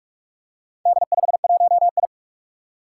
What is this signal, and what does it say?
Morse code, heard as a keyed tone, about 46 wpm, 700 Hz. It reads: D51I
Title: Four Challenging Letter-Number Combinations 45wpm